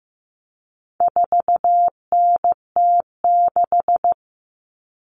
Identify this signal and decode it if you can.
Morse code, heard as a keyed tone, about 15 wpm, 700 Hz. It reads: 4NT6
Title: Common Cut Numbers - 4-Digit 15wpm